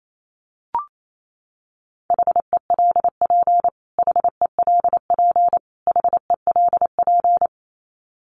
Morse code, heard as a keyed tone, about 28 wpm, 700 Hz. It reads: HELP HELP HELP
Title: Top 300 Words Repeated 28wpm